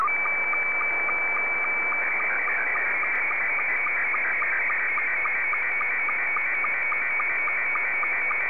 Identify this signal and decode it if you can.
SSTV (slow-scan television), Scottie 2.